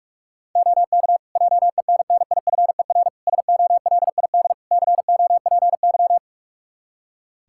Morse code, heard as a keyed tone, about 41 wpm, 700 Hz. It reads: OK JENNIFER SOLID COPY
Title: Practice QSO - Exchange 3 - More Info - 45wpm